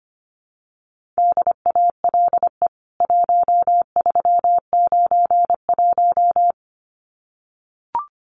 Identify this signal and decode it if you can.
Morse code, heard as a keyed tone, about 25 wpm, 700 Hz. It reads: DALE 1391
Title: CWT Exchange 25wpm